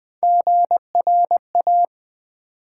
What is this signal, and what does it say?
Morse code, heard as a keyed tone, about 20 wpm, 700 Hz. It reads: GRA
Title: Build A Word - 3 Letters At A Time - 8-Letter Words - 20wpm